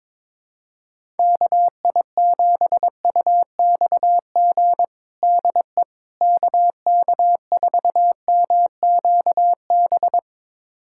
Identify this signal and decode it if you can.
Morse code, heard as a keyed tone, about 22 wpm, 700 Hz. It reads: KI7UXG DE KK4MQB
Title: USA Callsign DE Callsign - 6 Characters - 22wpm